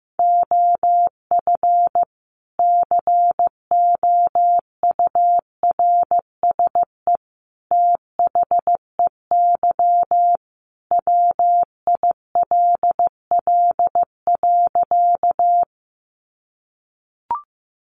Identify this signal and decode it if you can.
Morse code, heard as a keyed tone, about 15 wpm, 700 Hz. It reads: OF COURSE THEY WILL.
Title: Sentences from Top 500 Words 15wpm